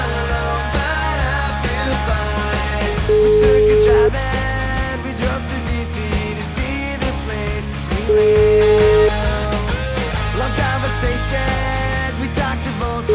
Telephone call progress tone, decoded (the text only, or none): ringback tone